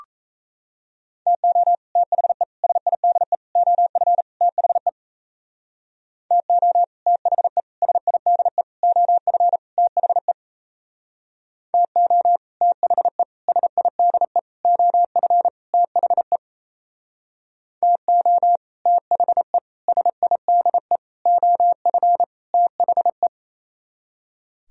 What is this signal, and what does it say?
Morse code, heard as a keyed tone, about 35 wpm, 700 Hz. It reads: TO THE SIDE OF THE TO THE SIDE OF THE TO THE SIDE OF THE TO THE SIDE OF THE